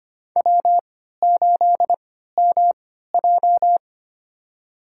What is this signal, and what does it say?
Morse code, heard as a keyed tone, about 25 wpm, spaced as for 14 wpm, 700 Hz. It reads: W8MJ
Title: USA Call Signs - 3x Character Spacing - 25wpm